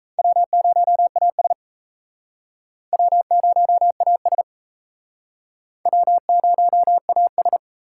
Morse code, heard as a keyed tone, about 37 wpm, 700 Hz. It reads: W0AS W0AS W0AS